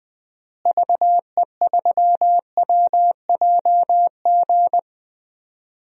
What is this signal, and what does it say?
Morse code, heard as a keyed tone, about 20 wpm, 700 Hz. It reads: VE3WJG
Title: DXCC-100 Call Signs - 6 Characters - Repeated 2x - 20wpm